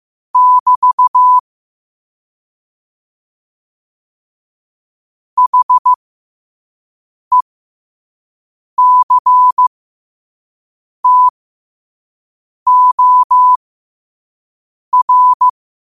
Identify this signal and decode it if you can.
Morse code, heard as a keyed tone, about 15 wpm, spaced as for 5 wpm, 1000 Hz. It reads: <BT> HECTOR
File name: qotdfiles / 5wpm